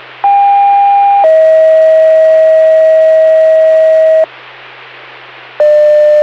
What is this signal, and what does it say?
Two-tone page, heard as a one-second tone, then a three-second tone, about 800 then 640 Hz. It is heard from 0.2 to 4.2 s.